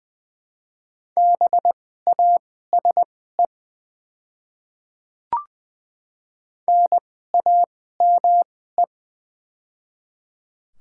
Morse code, heard as a keyed tone, about 20 wpm, spaced as for 14 wpm, 700 Hz. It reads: BASE NAME
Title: 2x Character Spacing - Top 300 Words - 20wpm